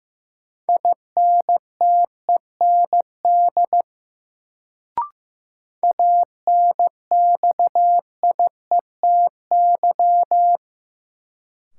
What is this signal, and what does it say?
Morse code, heard as a keyed tone, about 15 wpm, 700 Hz. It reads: INTEND ANXIETY
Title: Sentences from Top 1600 Words - New Word Drill Alt - 15wpm